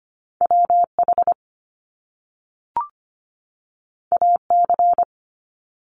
Morse code, heard as a keyed tone, about 25 wpm, 700 Hz. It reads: WH AC
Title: Top Two Letter Combinations in Words 25wpm